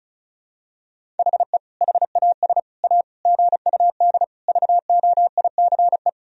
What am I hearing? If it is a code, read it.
Morse code, heard as a keyed tone, about 34 wpm, 700 Hz. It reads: HE HAS A GUD VOICE
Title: Sentences from Top 500 Words and Abbreviations 35wpm